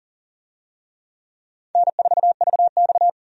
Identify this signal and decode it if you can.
Morse code, heard as a keyed tone, about 40 wpm, 700 Hz. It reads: N4VX
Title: USA Call Signs - 4 Characters - 40wpm